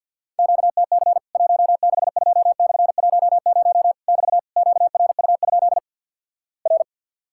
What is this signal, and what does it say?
Morse code, heard as a keyed tone, about 49 wpm, 700 Hz. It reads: QTC 162/10 <BT> QRU? R
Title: Worked All Europe DX Contest - QTC Exchange with Repeats - 50wpm